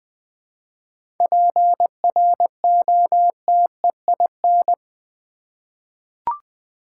Morse code, heard as a keyed tone, about 20 wpm, 700 Hz. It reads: PROTEIN
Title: 7 Letter Words - First 3 Letters Given - 20wpm